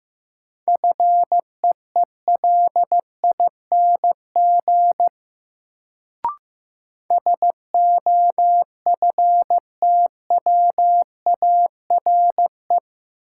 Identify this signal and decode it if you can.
Morse code, heard as a keyed tone, about 15 wpm, 700 Hz. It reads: FEELING SOFTWARE
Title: Sentences from Top 1000 Words - Review Part 3 - New Word Drill - 15wpm